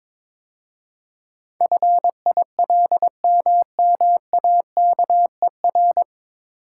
Morse code, heard as a keyed tone, about 22 wpm, 700 Hz. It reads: FILMMAKER
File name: 9 Letter Words - First 3 Letters Given - 22wpm